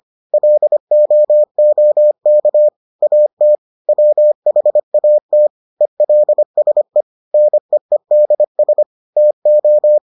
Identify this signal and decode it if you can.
Morse code, heard as a keyed tone, about 25 wpm, 600 Hz. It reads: LOOK AT WHAT ELSE NEEDS TO